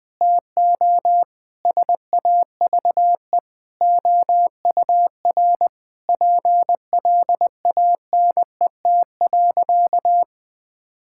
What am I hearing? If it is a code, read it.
Morse code, heard as a keyed tone, about 20 wpm, 700 Hz. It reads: TO SAVE OUR PLANET.
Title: Difficult Sentences with Hints - 20wpm